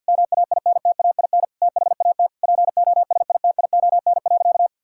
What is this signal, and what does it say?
Morse code, heard as a keyed tone, about 46 wpm, 690 Hz. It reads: MAINTAIN THAT POSITION.
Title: Sentences from Top 1000 Words - Review Part 5 - 50wpm